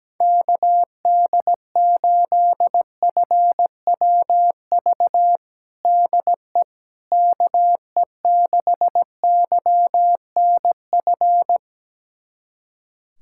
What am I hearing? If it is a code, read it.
Morse code, heard as a keyed tone, about 17 wpm, 700 Hz. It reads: KD8FWV DE KE6YNF
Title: USA Callsign DE Callsign 17wpm